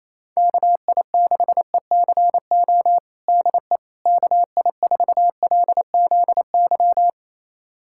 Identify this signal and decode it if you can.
Morse code, heard as a keyed tone, about 28 wpm, 700 Hz. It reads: KI6ECO DE KI4LZY
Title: USA Callsign DE Callsign - 6 Characters - 28wpm